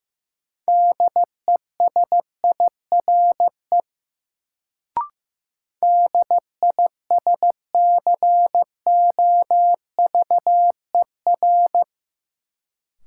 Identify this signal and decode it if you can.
Morse code, heard as a keyed tone, about 15 wpm, 700 Hz. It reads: DESIRE DISCOVER
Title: Sentences from Top 1100 Words - New Word Drill - 15wpm